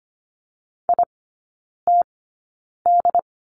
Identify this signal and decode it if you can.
Morse code, heard as a keyed tone, about 25 wpm, spaced as for 9 wpm, 690 Hz. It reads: ITD
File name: Lesson 14e - Number 2 - Three Characters 25wpm